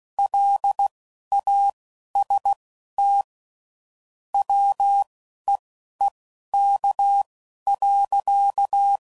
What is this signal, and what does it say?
Morse code, heard as a keyed tone, about 16 wpm, spaced as for 11 wpm, 800 Hz. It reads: LAST WEEK.
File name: CW Code Groups @10WPM / QSOs / QSO WD0FBT